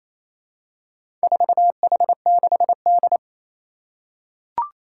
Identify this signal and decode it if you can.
Morse code, heard as a keyed tone, about 28 wpm, 700 Hz. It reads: 4H6D
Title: Four Challenging Letter-Number Combinations 28wpm